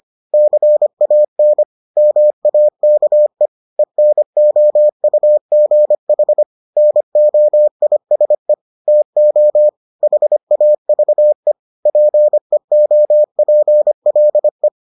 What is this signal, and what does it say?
Morse code, heard as a keyed tone, about 25 wpm, 600 Hz. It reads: CAN MAKE ENOUGH NOISE TO HAVE PEOPLE